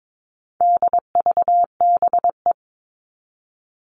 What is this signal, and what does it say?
Morse code, heard as a keyed tone, about 22 wpm, 700 Hz. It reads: DVBE
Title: All Cut Numbers - 4-Digit 22wpm